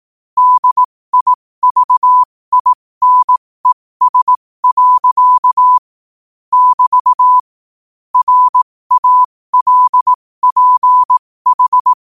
Morse code, heard as a keyed tone, about 18 wpm, spaced as for 15 wpm, 1000 Hz. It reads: DIVINES. <BT> RALPH
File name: qotdfiles / 15wpm